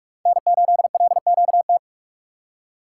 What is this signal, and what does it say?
Morse code, heard as a keyed tone, about 46 wpm, 690 Hz. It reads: N8LQT
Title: USA Call Signs - 5 Characters - 6x Character Spacing - 45wpm